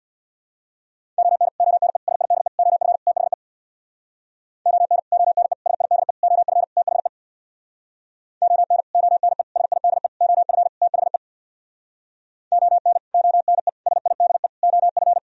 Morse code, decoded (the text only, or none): ON ONE SIDE OF THE ON ONE SIDE OF THE ON ONE SIDE OF THE ON ONE SIDE OF